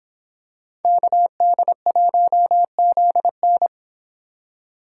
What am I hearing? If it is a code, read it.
Morse code, heard as a keyed tone, about 25 wpm, 700 Hz. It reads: KD1ZN